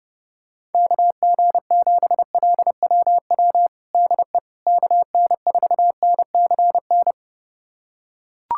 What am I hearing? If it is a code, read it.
Morse code, heard as a keyed tone, about 30 wpm, 700 Hz. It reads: KG7LWW DE KN4NCN
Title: USA Callsign DE Callsign 30wpm